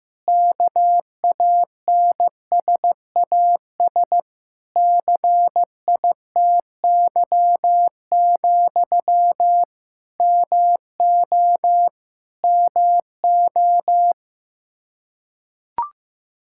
Morse code, heard as a keyed tone, about 15 wpm, 700 Hz. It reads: KANSAS CITY, MO MO
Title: Practice QSO - Exchange 1 - Beginning - 15wpm